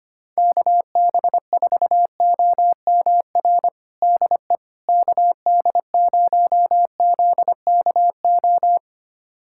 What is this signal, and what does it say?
Morse code, heard as a keyed tone, about 25 wpm, 700 Hz. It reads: KB4OMR DE KD0ZKO
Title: USA Callsign DE Callsign - 6 Characters - 25wpm